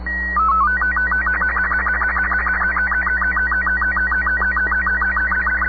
SSTV (slow-scan television), Robot 36.
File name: sat / AO-51 14Jul2010 SSTV